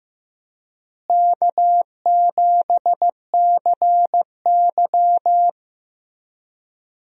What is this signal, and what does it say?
Morse code, heard as a keyed tone, about 15 wpm, 700 Hz. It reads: K7CY
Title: USA Call Signs - 4 Characters - Repeated 3x - 15wpm